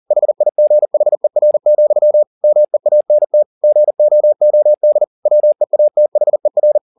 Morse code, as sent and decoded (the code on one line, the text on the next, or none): .... .. --. .... .-. --..-- / -- . .- -. - / --. --- --- -.. / .-- . .- - .... .-.
HIGHER, MEANT GOOD WEATHER